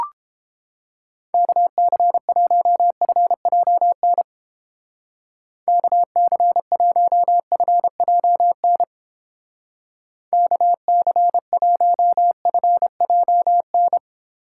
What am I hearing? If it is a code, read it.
Morse code, heard as a keyed tone, about 30 wpm, 700 Hz. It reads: KC1FJN KC1FJN KC1FJN